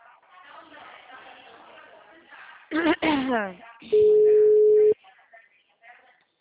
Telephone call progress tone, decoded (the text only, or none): ringback tone